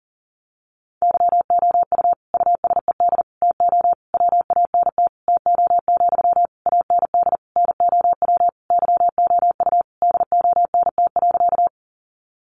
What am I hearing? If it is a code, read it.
Morse code, heard as a keyed tone, about 40 wpm, 700 Hz. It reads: YOU USED TO WANT TO, AND NOW YOU DONT.
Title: Sentences from Top 600 Words 40wpm